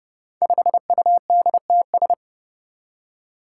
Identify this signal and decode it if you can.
Morse code, heard as a keyed tone, about 30 wpm, 700 Hz. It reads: 5UDTS
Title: Five Challenging Letter-Number Combinations 30wpm